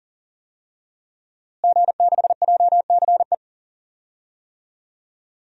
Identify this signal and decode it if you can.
Morse code, heard as a keyed tone, about 40 wpm, 700 Hz. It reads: G6JCE
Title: DXCC-100 Call Signs - 5 Characters - Repeated 2x - 40wpm